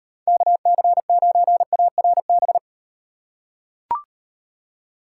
Morse code, heard as a keyed tone, about 38 wpm, 700 Hz. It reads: KC9ARB